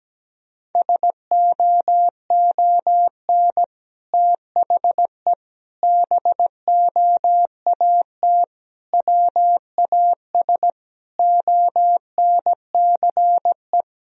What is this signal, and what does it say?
Morse code, heard as a keyed tone, about 17 wpm, 700 Hz. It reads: SOON THE BOAT WAS ONCE
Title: Sentences from Top 1000 Words - Review Part 3 - 17wpm